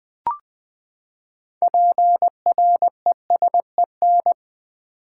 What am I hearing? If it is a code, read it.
Morse code, heard as a keyed tone, about 20 wpm, 700 Hz. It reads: PRESEN
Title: Build A Word - 6 Letters At A Time - 12-Letter Words - 20wpm